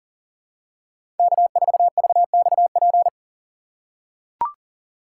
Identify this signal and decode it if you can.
Morse code, heard as a keyed tone, about 40 wpm, 700 Hz. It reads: K4VXP
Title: USA Call Signs - 5 Characters - 3x Character Spacing - 40wpm